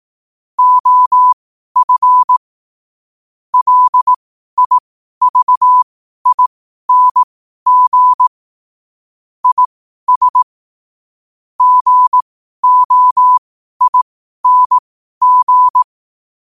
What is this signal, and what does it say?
Morse code, heard as a keyed tone, about 18 wpm, spaced as for 12 wpm, 1000 Hz. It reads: OF LIVING IS GOING
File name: qotdfiles / 12wpm